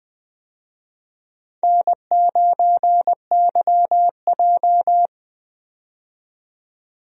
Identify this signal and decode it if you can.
Morse code, heard as a keyed tone, about 20 wpm, 700 Hz. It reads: N9YJ
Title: USA Call Signs - 4 Characters - 20wpm